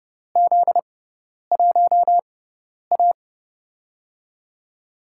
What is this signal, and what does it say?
Morse code, heard as a keyed tone, about 31 wpm, spaced as for 10 wpm, 700 Hz. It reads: Z1A
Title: USA Call Signs - 6x Character Spacing - 30wpm